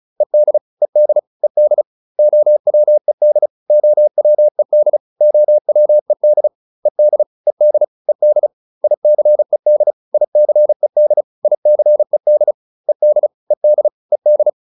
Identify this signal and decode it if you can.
Morse code, heard as a keyed tone, about 34 wpm, 600 Hz. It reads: ED ED ED OWED OWED OWED ED ED ED ICED ICED ICED ED ED ED